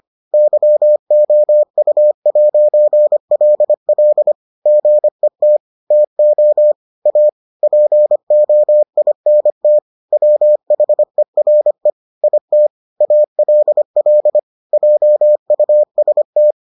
Morse code, heard as a keyed tone, about 25 wpm, 600 Hz. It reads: YOU'LL GET TO A POINT WHERE IT ALL JUST